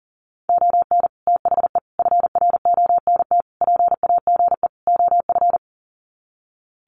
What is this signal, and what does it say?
Morse code, heard as a keyed tone, about 40 wpm, 700 Hz. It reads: ON THE FRONT PAGE OF
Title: Sets of 5 Words Encore 40wpm